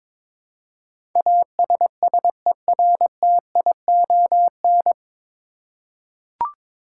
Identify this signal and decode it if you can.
Morse code, heard as a keyed tone, about 22 wpm, 700 Hz. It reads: ASSERTION
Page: Morse Code Ninja practice 22wpm